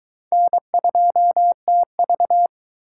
Morse code, heard as a keyed tone, about 21 wpm, 700 Hz. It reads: N2TV